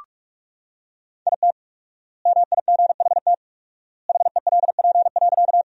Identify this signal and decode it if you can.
Morse code, heard as a keyed tone, about 44 wpm, 700 Hz. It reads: IT MIGHT HELP.
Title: 4x Word-Spacing - Sentences from Top 300 Words - 45wpm